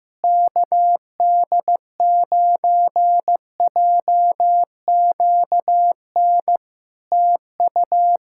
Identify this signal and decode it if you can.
Morse code, heard as a keyed tone, about 15 wpm, 700 Hz. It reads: KD9JQN TU
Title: Practice QSO - Exchange 4 - Ending - 15wpm